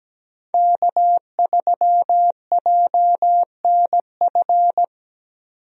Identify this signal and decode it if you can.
Morse code, heard as a keyed tone, about 17 wpm, 700 Hz. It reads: K3JNF
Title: USA Call Signs - 5 Characters - 17wpm